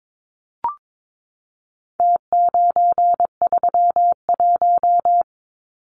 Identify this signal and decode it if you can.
Morse code, heard as a keyed tone, about 22 wpm, 700 Hz. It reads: T931
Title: Numbers - Contest Style - 4-Digit 22wpm